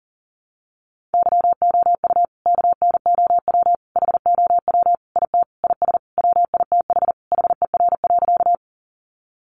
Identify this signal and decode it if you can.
Morse code, heard as a keyed tone, about 40 wpm, 700 Hz. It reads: YOU KNOW HOW IT IS WITH HER.
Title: Sentences from Top 100 Words 40wpm